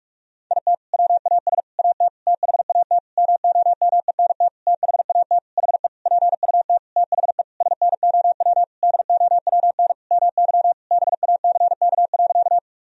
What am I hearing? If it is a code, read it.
Morse code, heard as a keyed tone, about 47 wpm, 700 Hz. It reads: IT WAS AT THAT MOMENT THAT HE PUT THE SNOW DOWN MY BACK.
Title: Sentences from Top 1000 Words - Review Part 3 - 45wpm